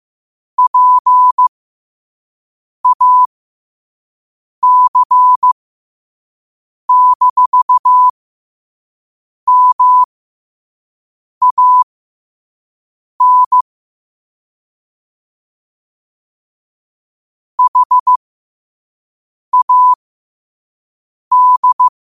Morse code, heard as a keyed tone, about 15 wpm, spaced as for 5 wpm, 1000 Hz. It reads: PAC-MAN HAD
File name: qotdfiles / 5wpm